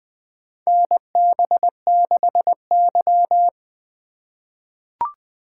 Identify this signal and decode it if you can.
Morse code, heard as a keyed tone, about 20 wpm, 700 Hz. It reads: NB6Y
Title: USA Call Signs - 4 Characters - 3x Character Spacing - 20wpm